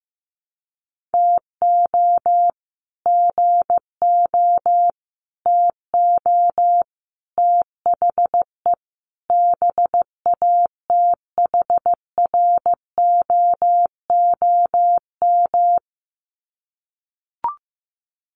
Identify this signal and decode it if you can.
Morse code, heard as a keyed tone, about 15 wpm, 700 Hz. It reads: TO GO TO THE BATHROOM
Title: Sets of 5 Words Encore 15wpm